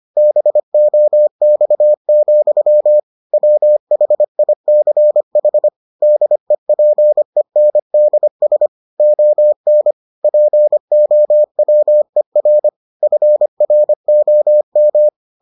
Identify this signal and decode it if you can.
Morse code, heard as a keyed tone, about 25 wpm, 600 Hz. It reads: BOX, WHICH DEPENDS ON POWER FROM